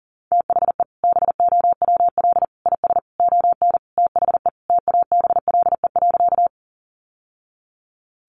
Morse code, heard as a keyed tone, about 40 wpm, 700 Hz. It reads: THE BOWL IS ON THE TABLE.
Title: Sentences from Top 1200 Words 40wpm